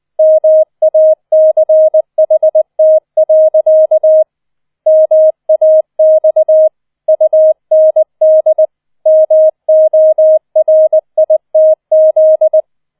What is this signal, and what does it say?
Morse code, heard as a keyed tone, about 20 wpm, 630 Hz. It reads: MACHT. MAX UND MORITZ